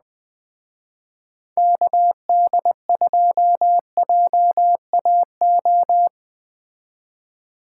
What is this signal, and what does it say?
Morse code, heard as a keyed tone, about 19 wpm, 700 Hz. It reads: KD2JAO